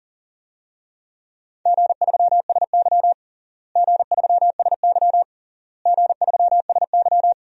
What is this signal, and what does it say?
Morse code, heard as a keyed tone, about 40 wpm, 700 Hz. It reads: G3SY G3SY G3SY
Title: DXCC-100 Call Signs - 4 Characters - Repeated 3x - 40wpm